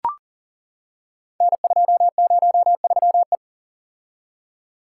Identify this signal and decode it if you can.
Morse code, heard as a keyed tone, about 40 wpm, 700 Hz. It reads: N203E
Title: Code Groups - 5 Letter:Numbers - 40wpm